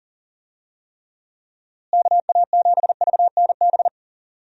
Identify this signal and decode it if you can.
Morse code, heard as a keyed tone, about 39 wpm, 700 Hz. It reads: KA7VNB
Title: USA Call Signs - 6 Characters - Repeated 3x - 40wpm